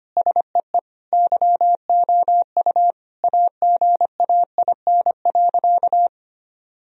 Morse code, heard as a keyed tone, about 25 wpm, 700 Hz. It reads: SEE YOU AGAIN.
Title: Sentences from Top 1000 Words - Review Part 1 - 25wpm